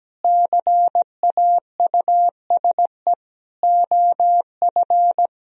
Morse code, heard as a keyed tone, about 17 wpm, 700 Hz. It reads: CAUSE OF
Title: Sentences from Top 1000 Words - Review Part 1 - 17wpm